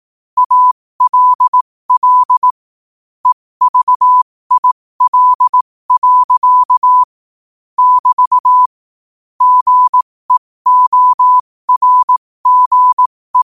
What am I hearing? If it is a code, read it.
Morse code, heard as a keyed tone, about 18 wpm, spaced as for 15 wpm, 1000 Hz. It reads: ALL EVIL. <BT> GEORGE
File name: qotdfiles / 15wpm